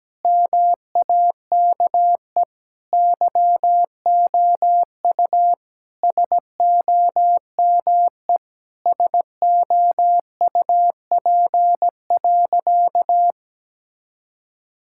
Morse code, heard as a keyed tone, about 17 wpm, 700 Hz. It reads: MAKE YOU SOME SOUP.
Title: Sentences from Top 1000 Words - Review Part 6 - 17wpm